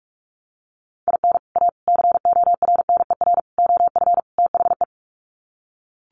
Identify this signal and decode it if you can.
Morse code, heard as a keyed tone, about 44 wpm, 700 Hz. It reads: IN A CORNER OF THE
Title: Sets of 5 Words Encore2 45wpm